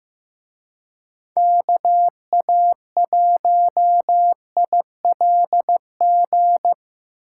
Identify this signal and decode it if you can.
Morse code, heard as a keyed tone, about 18 wpm, 700 Hz. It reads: KA1ILG
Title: Speed Racing - USA 2x3 Call Signs 15wpm